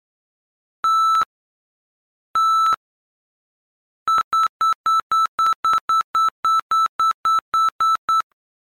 There is no code.